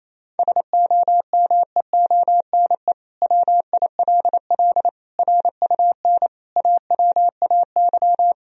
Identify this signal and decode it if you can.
Morse code, heard as a keyed tone, about 27 wpm, 700 Hz. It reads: SOMEONE WILL RUN AWAY
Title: Sentences from Top 1000 Words - Review Part 1 - 28wpm